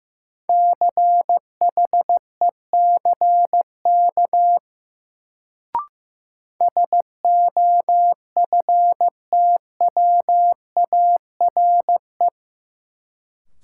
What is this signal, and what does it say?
Morse code, heard as a keyed tone, about 15 wpm, 700 Hz. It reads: CHECK SOFTWARE
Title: Sentences from Top 1000 Words - Review Part 3 - New Word Drill - 15wpm